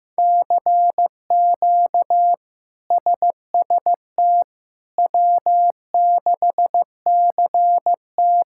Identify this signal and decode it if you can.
Morse code, heard as a keyed tone, about 15 wpm, 700 Hz. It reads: CQ SST W6CT
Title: K1USN SST QSO Practice - 15wpm